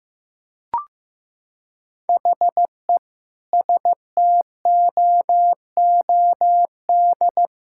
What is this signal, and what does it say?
Morse code, heard as a keyed tone, about 15 wpm, 700 Hz. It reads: HE STOOD
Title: Sentences from Top 1600 Words 15wpm